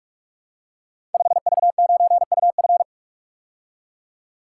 Morse code, heard as a keyed tone, about 44 wpm, 700 Hz. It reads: 5V9UF
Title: Code Groups - 5 Letter:Numbers - 45wpm